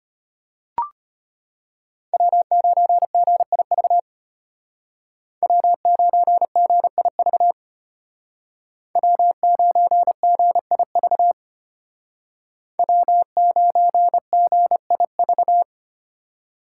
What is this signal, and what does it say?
Morse code, heard as a keyed tone, about 31 wpm, 700 Hz. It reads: W9GIV W9GIV W9GIV W9GIV
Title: Speed Racing - USA Call Signs 25wpm